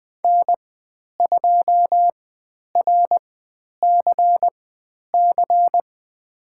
Morse code, heard as a keyed tone, about 20 wpm, spaced as for 10 wpm, 700 Hz. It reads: N2RCC
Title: 102 Lesson 17g - Letter u - Call Signs 20wpm